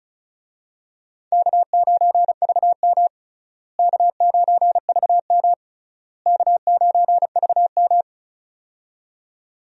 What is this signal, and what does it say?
Morse code, heard as a keyed tone, about 34 wpm, 700 Hz. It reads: K9VM K9VM K9VM
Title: USA Call Signs - 4 Characters - Repeated 3x - 35wpm